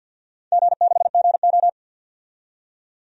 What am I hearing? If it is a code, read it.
Morse code, heard as a keyed tone, about 51 wpm, 700 Hz. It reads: G6GO
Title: DXCC-100 Call Signs - 4 Characters - 50wpm